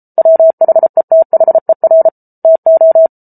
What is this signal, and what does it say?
Morse code, heard as a keyed tone, about 34 wpm, 650 Hz. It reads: WHETHER TO